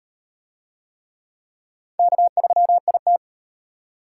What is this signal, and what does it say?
Morse code, heard as a keyed tone, about 36 wpm, 700 Hz. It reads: K3IT